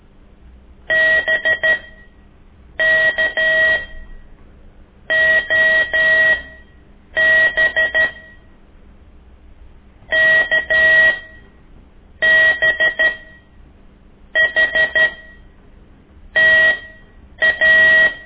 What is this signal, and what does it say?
Morse code, heard as a keyed tone, about 13 wpm, spaced as for 6 wpm, 1900 Hz. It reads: BKOB KBHTA
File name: ik1weg2 / CORSI CW